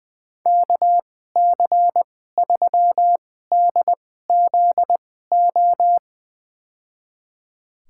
Morse code, heard as a keyed tone, about 20 wpm, spaced as for 14 wpm, 700 Hz. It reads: KC3DZO
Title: USA Call Signs - 6 Characters - 2x Character Spacing - 20wpm